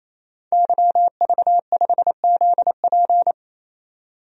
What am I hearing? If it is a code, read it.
Morse code, heard as a keyed tone, about 27 wpm, 700 Hz. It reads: YV5ZP
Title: DXCC-100 Call Signs - 5 Characters - 28wpm